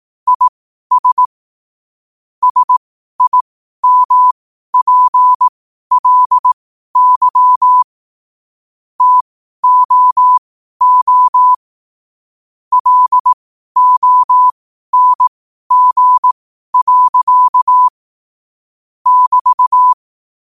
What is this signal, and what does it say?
Morse code, heard as a keyed tone, about 18 wpm, spaced as for 12 wpm, 1000 Hz. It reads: IS SIMPLY TOO LONG. <BT>
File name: qotdfiles / 12wpm